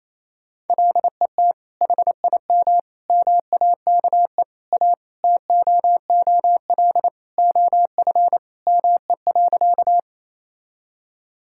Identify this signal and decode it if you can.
Morse code, heard as a keyed tone, about 28 wpm, 700 Hz. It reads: LET HIM MAKE A TOOL OF ME.
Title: Sentences from Top 1000 Words - Review Part 6 - 28wpm